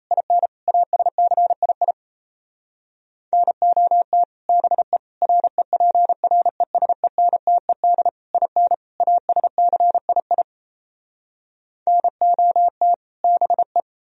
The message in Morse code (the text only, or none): IN ASCII NOT BE REPRESENTED IN ASCII NOT BE